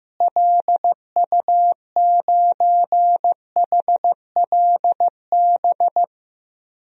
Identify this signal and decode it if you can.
Morse code, heard as a keyed tone, about 15 wpm, 700 Hz. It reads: LU9HLB
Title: DXCC-100 Call Signs - 6 Characters - Repeated 2x - 15wpm